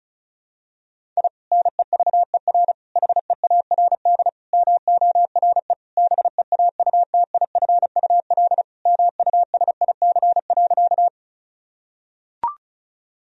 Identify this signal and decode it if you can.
Morse code, heard as a keyed tone, about 34 wpm, 700 Hz. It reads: I NEVER HEARD MORE BEAUTIFUL MUSIC.
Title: Sentences from Top 1000 Words - Review Part 1 - 35wpm